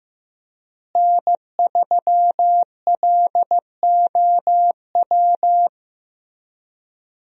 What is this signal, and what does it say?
Morse code, heard as a keyed tone, about 15 wpm, 700 Hz. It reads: N3LOW
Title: USA Call Signs - 5 Characters - Repeated 3x - 15wpm